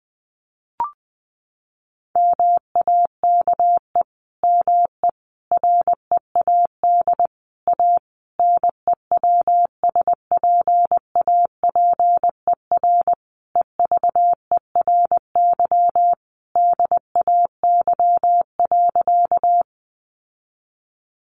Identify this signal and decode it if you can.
Morse code, heard as a keyed tone, about 20 wpm, 700 Hz. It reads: MAKE ME READ A NEWSPAPER EVERY DAY.
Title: Sentences from Top 1100 Words 20wpm